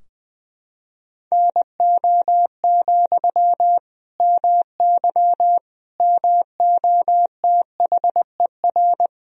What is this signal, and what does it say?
Morse code, heard as a keyed tone, about 20 wpm, 700 Hz. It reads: NO, MY MOTHER
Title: Sentences from Top 1000 Words - Review Part 2 - 20wpm